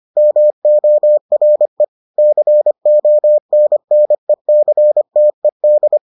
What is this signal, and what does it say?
Morse code, heard as a keyed tone, about 25 wpm, 600 Hz. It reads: MORE CONNECTED